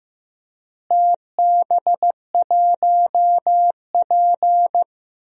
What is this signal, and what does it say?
Morse code, heard as a keyed tone, about 16 wpm, 700 Hz. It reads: TB1P
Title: DXCC-100 Call Signs - 4 Characters - Repeated 3x - 15wpm